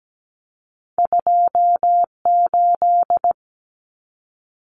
Morse code, heard as a keyed tone, about 17 wpm, 700 Hz. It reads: 28
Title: Numbers - 2-Digit 17wpm